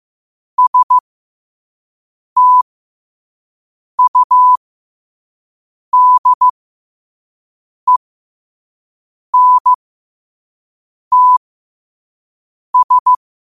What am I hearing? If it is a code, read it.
Morse code, heard as a keyed tone, about 15 wpm, spaced as for 5 wpm, 1000 Hz. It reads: STUDENTS
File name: qotdfiles / 5wpm